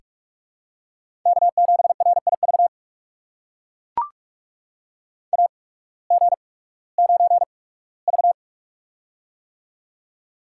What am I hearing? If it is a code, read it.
Morse code, heard as a keyed tone, about 43 wpm, spaced as for 21 wpm, 700 Hz. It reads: K7RIV AG9V
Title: USA Call Signs - 8x Character Spacing - 45wpm